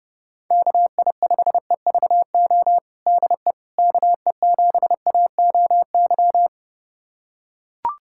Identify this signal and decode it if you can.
Morse code, heard as a keyed tone, about 32 wpm, 700 Hz. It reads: KI5EVO DE KE7AOY
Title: USA Callsign DE Callsign 30wpm